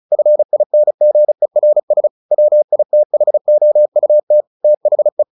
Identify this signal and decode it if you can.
Morse code, heard as a keyed tone, about 34 wpm, 600 Hz. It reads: FINGERS WITHOUT THE
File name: the old man and the sea / the old man and the sea 1 35wpm